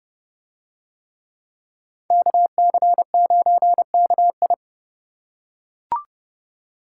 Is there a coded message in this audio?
Yes: Morse code from 2.1 to 4.5 s, heard as a keyed tone, about 30 wpm, 700 Hz. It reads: KC9KI